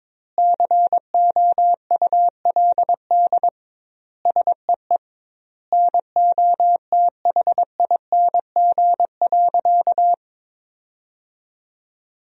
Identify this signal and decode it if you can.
Morse code, heard as a keyed tone, about 22 wpm, 700 Hz. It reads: COULD SEE NOTHING.
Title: 2x Word-Spacing - Sentences from Top 500 Words - 22wpm